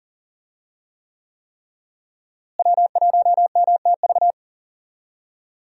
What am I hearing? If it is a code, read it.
Morse code, heard as a keyed tone, about 40 wpm, 700 Hz. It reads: W1MTV
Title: USA Call Signs - 5 Characters - Repeated 3x - 40wpm